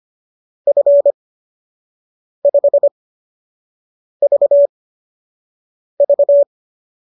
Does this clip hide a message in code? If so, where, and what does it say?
Yes: Morse code from 0.7 to 6.4 s, heard as a keyed tone, about 25 wpm, spaced as for 6 wpm, 570 Hz. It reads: F5VV